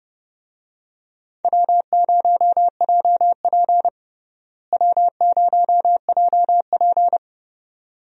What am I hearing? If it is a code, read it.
Morse code, heard as a keyed tone, about 31 wpm, 700 Hz. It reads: W0JP W0JP
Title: USA Call Signs - 4 Characters - Repeated 2x - 30wpm